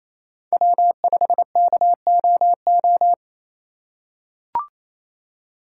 Morse code, heard as a keyed tone, about 28 wpm, 700 Hz. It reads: W5KOO
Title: USA Call Signs - 5 Characters - 2x Character Spacing - 28wpm